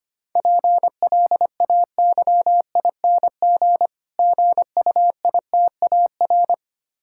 Morse code, heard as a keyed tone, about 26 wpm, 700 Hz. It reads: PLAYING GUITAR
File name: Easy Words - Speed-Racing with Hints - 25wpm